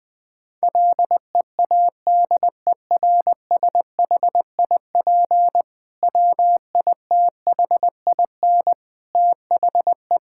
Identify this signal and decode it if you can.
Morse code, heard as a keyed tone, about 20 wpm, 700 Hz. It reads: LEADERSHIP WITHIN THE
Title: Sentences from Top 1600 Words 20wpm